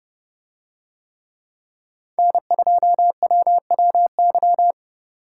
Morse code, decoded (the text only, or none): N2WWY